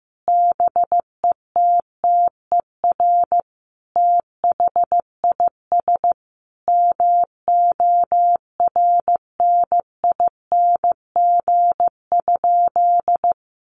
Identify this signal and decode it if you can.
Morse code, heard as a keyed tone, about 14 wpm, 700 Hz. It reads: BETTER THIS MORNING?
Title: Sentences from Top 1200 Words 15wpm